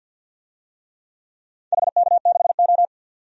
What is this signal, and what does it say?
Morse code, heard as a keyed tone, about 47 wpm, 700 Hz. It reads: HK6O
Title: DXCC-100 Call Signs - 4 Characters - 50wpm